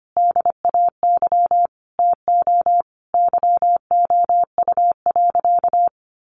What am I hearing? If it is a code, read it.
Morse code, heard as a keyed tone, about 25 wpm, 700 Hz. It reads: DAY TO YOU.
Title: Sentences from Top 200 Words 25wpm